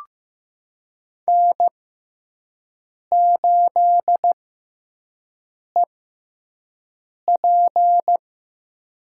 Morse code, heard as a keyed tone, about 15 wpm, spaced as for 5 wpm, 700 Hz. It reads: N8EP
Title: USA Call Signs - 4 Characters - 6x Character Spacing - 15wpm